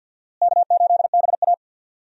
Morse code, heard as a keyed tone, about 46 wpm, 700 Hz. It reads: K8BA
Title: USA Call Signs - 4 Characters - Repeated 3x - 50wpm